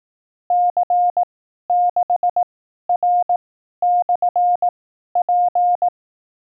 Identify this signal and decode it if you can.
Morse code, heard as a keyed tone, about 18 wpm, spaced as for 12 wpm, 710 Hz. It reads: C6R/P
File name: 12WPM test